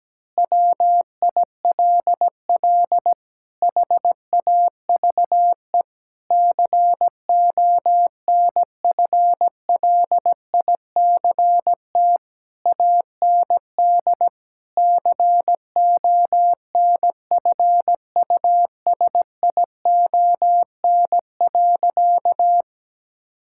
Morse code, heard as a keyed tone, about 17 wpm, 700 Hz. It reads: WILL HAVE CONFLICT AND CONFUSION.
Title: Sentences from Top 1600 Words 17wpm